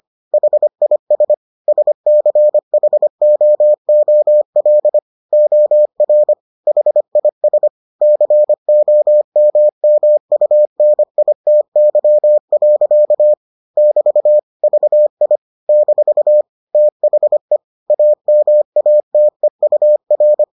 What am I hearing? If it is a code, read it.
Morse code, heard as a keyed tone, about 25 wpm, 600 Hz. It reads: HIS SCHOOL OR HIS COMMUNITY. <BT> VI - THE AMATEUR